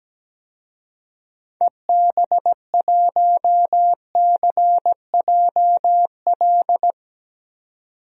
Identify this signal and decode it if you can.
Morse code, heard as a keyed tone, about 17 wpm, 700 Hz. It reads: EB1CJL
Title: DXCC-100 Call Signs - 6 Characters - 17wpm